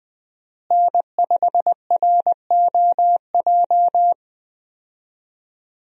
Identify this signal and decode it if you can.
Morse code, heard as a keyed tone, about 20 wpm, 700 Hz. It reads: N5ROJ
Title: USA Call Signs - 5 Characters - Repeated 2x - 20wpm